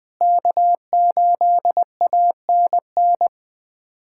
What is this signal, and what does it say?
Morse code, heard as a keyed tone, about 21 wpm, 700 Hz. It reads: K8ANN
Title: USA Callsign DE Callsign - 5 Characters - 20wpm